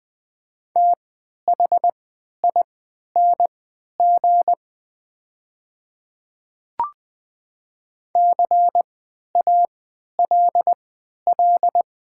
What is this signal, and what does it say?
Morse code, heard as a keyed tone, about 20 wpm, spaced as for 11 wpm, 700 Hz. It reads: THING CALL
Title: 3x Character Spacing - Top 100 Words - 20wpm